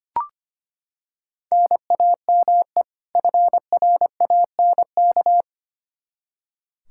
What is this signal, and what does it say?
Morse code, heard as a keyed tone, about 25 wpm, 700 Hz. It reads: NAME FRANK
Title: Practice QSO - Exchange 1 - Beginning - 25wpm